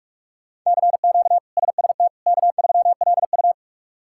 Morse code, heard as a keyed tone, about 46 wpm, 700 Hz. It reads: CQ SST K3LU
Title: K1USN SST QSO Practice - 45wpm